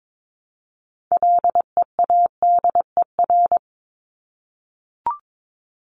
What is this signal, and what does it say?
Morse code, heard as a keyed tone, about 22 wpm, 700 Hz. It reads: LEADER
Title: Top 1501-1750 Words 22wpm